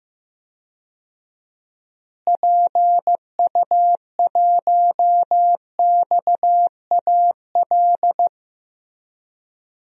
Morse code, heard as a keyed tone, about 15 wpm, 700 Hz. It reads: PU1XAL
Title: DXCC-100 Call Signs - 6 Characters - Repeated 3x - 15wpm